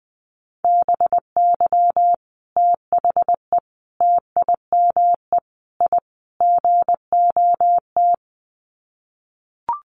Morse code, heard as a keyed tone, about 20 wpm, 700 Hz. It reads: BY THE TIME I GOT
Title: Sets of 5 Words Encore2 20wpm